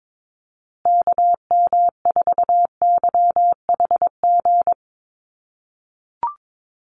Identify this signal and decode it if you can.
Morse code, heard as a keyed tone, about 22 wpm, 700 Hz. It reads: KM4YHG
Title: USA Call Signs Repeated 2x 22wpm